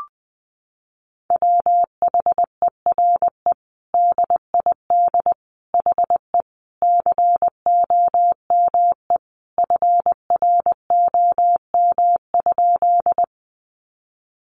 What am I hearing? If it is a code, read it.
Morse code, heard as a keyed tone, about 20 wpm, 700 Hz. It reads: WHERE DID HE COME FROM?
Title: Sentences from Top 200 Words 20wpm